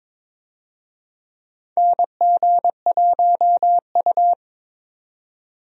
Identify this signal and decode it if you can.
Morse code, heard as a keyed tone, about 21 wpm, 700 Hz. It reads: NG1U
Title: USA Call Signs - 4 Characters - 22wpm